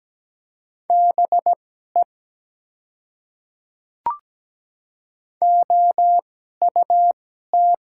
Morse code, heard as a keyed tone, about 17 wpm, spaced as for 12 wpm, 700 Hz. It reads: BE OUT
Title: 2x Character Spacing - Top 100 Words - 17wpm